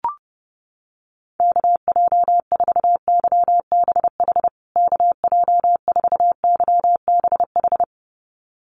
Morse code, heard as a keyed tone, about 28 wpm, 700 Hz. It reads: KJ4YBH KJ4YBH
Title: USA Call Signs Repeated 2x 30wpm